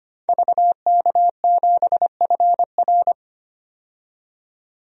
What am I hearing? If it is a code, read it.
Morse code, heard as a keyed tone, about 25 wpm, 700 Hz. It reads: VK7FR
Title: DXCC-100 Call Signs - 5 Characters - 25wpm